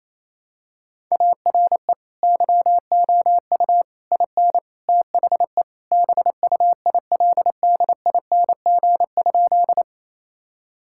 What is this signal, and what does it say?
Morse code, heard as a keyed tone, about 27 wpm, 700 Hz. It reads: ARE YOU IN THE BUILDING?
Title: Sentences from Top 1000 Words - Review Part 5 - 28wpm